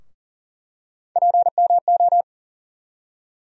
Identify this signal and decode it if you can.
Morse code, heard as a keyed tone, about 40 wpm, 700 Hz. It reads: PMO
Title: USA Call Sign Postfixes - 3 Letters 40wpm